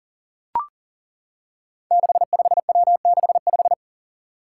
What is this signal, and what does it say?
Morse code, heard as a keyed tone, about 40 wpm, 700 Hz. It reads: 65W65
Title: Five Challenging Letter-Number Combinations 40wpm